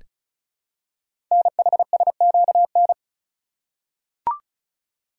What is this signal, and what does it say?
Morse code, heard as a keyed tone, about 36 wpm, 700 Hz. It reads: NHSQN
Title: Code Groups - 5 Letter:Numbers - 35wpm